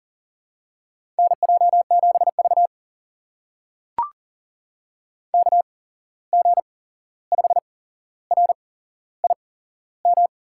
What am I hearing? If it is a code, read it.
Morse code, heard as a keyed tone, about 40 wpm, spaced as for 15 wpm, 700 Hz. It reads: NJ7V KG5RIM
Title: USA Call Signs - 8x Character Spacing - 40wpm